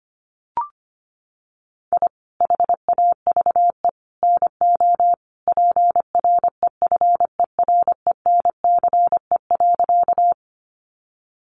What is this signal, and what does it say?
Morse code, heard as a keyed tone, about 25 wpm, 700 Hz. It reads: I HAVE NO PREFERENCE.
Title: Sentences from Top 1500 Words 25wpm